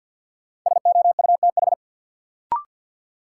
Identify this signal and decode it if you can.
Morse code, heard as a keyed tone, about 48 wpm, 700 Hz. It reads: SOUTH
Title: Sentences from Top 400 Words - New Word Drill - 50wpm